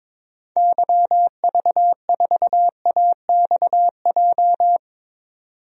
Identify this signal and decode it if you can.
Morse code, heard as a keyed tone, about 22 wpm, 700 Hz. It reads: YV4AXJ
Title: DXCC-100 Call Signs - 6 Characters - 22wpm